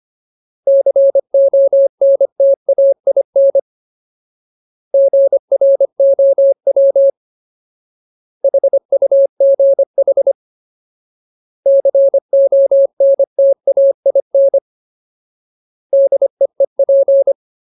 Morse code, heard as a keyed tone, about 25 wpm, 550 Hz. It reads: CONTAIN GROW HUGH CONTAIN DEEP
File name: Ultimate Word List / 25WPM Ultimate